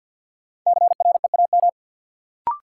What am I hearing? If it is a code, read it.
Morse code, heard as a keyed tone, about 52 wpm, 700 Hz. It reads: CREAM
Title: Sentences from Top 1500 Words - New Word Drill - 50wpm